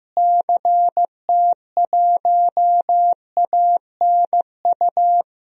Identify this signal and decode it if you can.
Morse code, heard as a keyed tone, about 16 wpm, 700 Hz. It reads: CT1ANU
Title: DXCC-100 Call Signs - 6 Characters - Repeated 3x - 15wpm